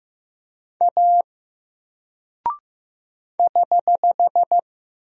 Morse code, heard as a keyed tone, about 15 wpm, 700 Hz. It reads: A <HH>
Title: Single Character (Letter:Number:Punctuation:Prosign) 15wpm